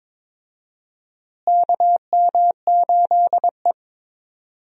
Morse code, heard as a keyed tone, about 22 wpm, 700 Hz. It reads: KM8E
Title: USA Call Signs - 4 Characters - Repeated 2x - 22wpm